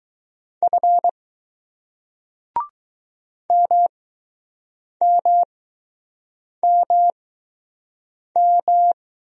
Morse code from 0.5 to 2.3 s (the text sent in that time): F